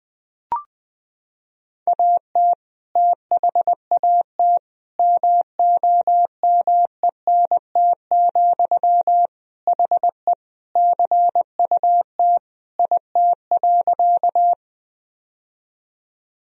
Morse code, heard as a keyed tone, about 20 wpm, 700 Hz. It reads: AT THAT MOMENT, HE CUT IT.
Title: Sentences from Top 1000 Words - Review Part 1 - 20wpm